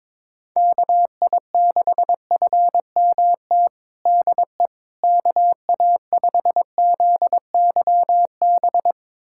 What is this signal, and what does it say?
Morse code, heard as a keyed tone, about 22 wpm, 700 Hz. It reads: KI6FMT DE KA5ZYB
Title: USA Callsign DE Callsign - 6 Characters - 22wpm